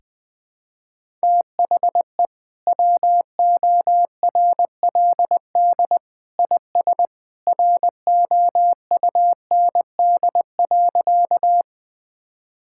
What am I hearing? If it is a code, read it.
Morse code, heard as a keyed tone, about 20 wpm, 700 Hz. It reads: THE WORLD IS ROUND.
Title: Sentences with 2-Word Phrases - 20wpm